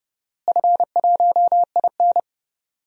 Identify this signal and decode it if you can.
Morse code, heard as a keyed tone, about 30 wpm, 700 Hz. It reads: F1IN
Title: DXCC-100 Call Signs - 4 Characters - Repeated 3x - 30wpm